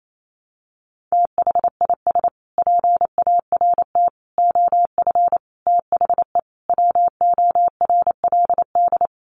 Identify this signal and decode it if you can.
Morse code, heard as a keyed tone, about 28 wpm, 700 Hz. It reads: THIS PART OF THE WORLD
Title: Sets of 5 Words Encore2 28wpm